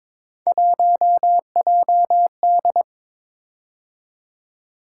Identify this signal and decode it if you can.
Morse code, heard as a keyed tone, about 23 wpm, 700 Hz. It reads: 1JD
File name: practice / 22wpm / Three Letter-Number Combinations - 22wpm